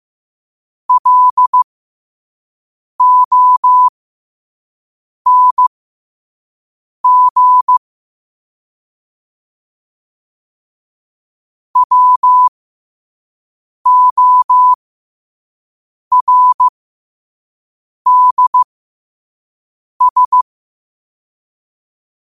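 Morse code, heard as a keyed tone, about 15 wpm, spaced as for 5 wpm, 1000 Hz. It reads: LONG WORDS
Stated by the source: Morse code sent at 5 wpm.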